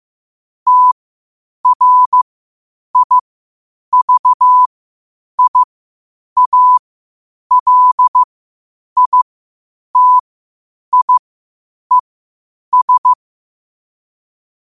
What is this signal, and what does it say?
Morse code, heard as a keyed tone, about 15 wpm, spaced as for 8 wpm, 1000 Hz. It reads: TRIVIALITIES
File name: qotdfiles / 8wpm